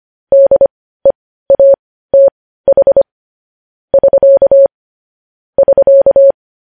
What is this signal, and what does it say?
Morse code, heard as a keyed tone, about 25 wpm, spaced as for 15 wpm, 550 Hz. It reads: DEATH <SK> <SK>
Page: News Headlines in Morse Code at 15 WPM